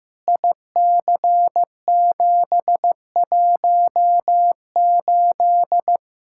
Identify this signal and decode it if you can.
Morse code, heard as a keyed tone, about 15 wpm, 700 Hz. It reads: IC718
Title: Practice QSO - Exchange 2 - Info - 15wpm